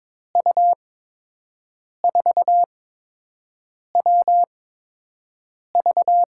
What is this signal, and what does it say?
Morse code, heard as a keyed tone, about 22 wpm, spaced as for 6 wpm, 700 Hz. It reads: U4WV
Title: USA Call Signs - 5 Characters - 8x Character Spacing - 22wpm